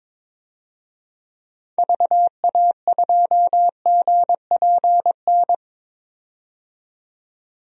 Morse code, heard as a keyed tone, about 22 wpm, 700 Hz. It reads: VA2GPN
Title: DXCC-100 Call Signs - 6 Characters - 22wpm